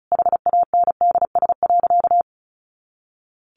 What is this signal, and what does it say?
Morse code, heard as a keyed tone, about 34 wpm, 700 Hz. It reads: HANDS.
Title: Easy Phrases 35wpm